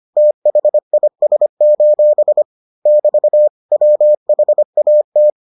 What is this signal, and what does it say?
Morse code, heard as a keyed tone, about 25 wpm, 600 Hz. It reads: THIS: <BT> WHAT